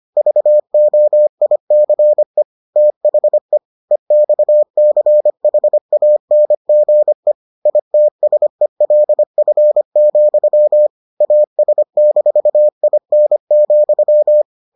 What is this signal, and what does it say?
Morse code, heard as a keyed tone, about 25 wpm, 600 Hz. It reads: VOICE THE EXCHANGE ITSELF, AS-IN,